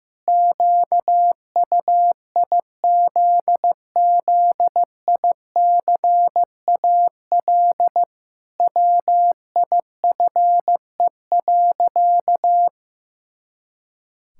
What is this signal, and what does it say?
Morse code, heard as a keyed tone, about 16 wpm, 700 Hz. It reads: QUIZZICAL WIFE.
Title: Pangram Challenge v2 - 15wpm